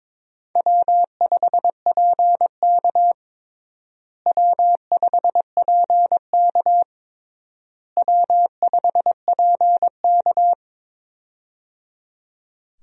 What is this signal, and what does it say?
Morse code, heard as a keyed tone, about 22 wpm, 700 Hz. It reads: W5PK W5PK W5PK
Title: USA Call Signs - 4 Characters - Repeated 3x - 22wpm